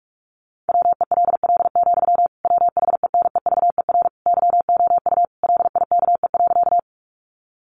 Morse code, heard as a keyed tone, about 44 wpm, 700 Hz. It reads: WELL, WHENEVER YOU LIKE.
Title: Sentences from Top 1200 Words 45wpm